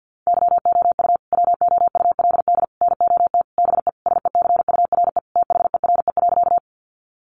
Morse code, heard as a keyed tone, about 50 wpm, 700 Hz. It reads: YOU WOULD NOT BE SECURE THERE.
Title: Sentences from Top 1200 Words 50wpm